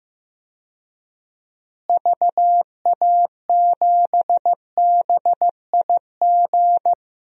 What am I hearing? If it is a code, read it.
Morse code, heard as a keyed tone, about 15 wpm, 700 Hz. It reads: VA7BIG
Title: DXCC-100 Call Signs - 6 Characters - Repeated 3x - 15wpm